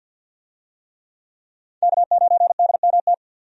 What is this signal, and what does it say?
Morse code, heard as a keyed tone, about 51 wpm, 700 Hz. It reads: K9DMT
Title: USA Call Signs - 5 Characters - 50wpm